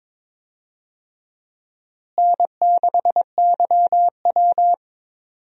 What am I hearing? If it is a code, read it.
Morse code, heard as a keyed tone, about 22 wpm, 700 Hz. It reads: N6YW
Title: USA Call Signs - 4 Characters - Repeated 2x - 22wpm